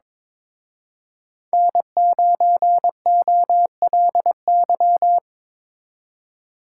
Morse code, heard as a keyed tone, about 25 wpm, 700 Hz. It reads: N9OLY
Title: Speed Racing - USA Call Signs 22wpm